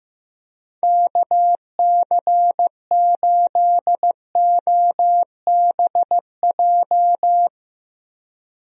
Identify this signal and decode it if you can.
Morse code, heard as a keyed tone, about 16 wpm, 700 Hz. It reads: KC8OBJ
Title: USA Call Signs - 6 Characters - Repeated 2x - 15wpm